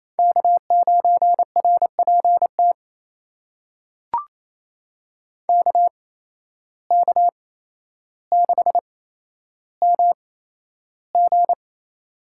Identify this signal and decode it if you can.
Morse code, heard as a keyed tone, about 27 wpm, spaced as for 12 wpm, 700 Hz. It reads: K9RPT KK6MG
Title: USA Call Signs - 5 Characters - 8x Character Spacing - 28wpm